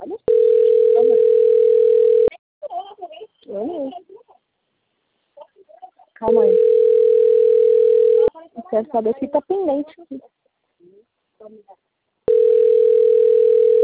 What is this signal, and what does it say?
Ringback tone (North America), 440 + 480 Hz, with speech.
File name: out- - - - - .13